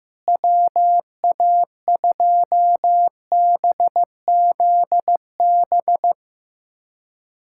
Morse code, heard as a keyed tone, about 15 wpm, 700 Hz. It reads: WA2BZB
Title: USA Callsign DE Callsign - 6 Characters - 15wpm